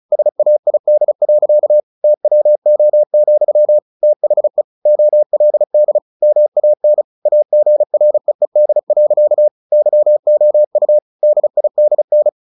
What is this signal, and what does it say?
Morse code, heard as a keyed tone, about 34 wpm, 600 Hz. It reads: SAID. TWO, THE OLD MAN AGREED. YOU DIDN
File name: the old man and the sea / the old man and the sea 1 35wpm